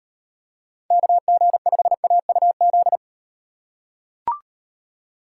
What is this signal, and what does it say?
Morse code, heard as a keyed tone, about 34 wpm, 700 Hz. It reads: KG5AUZ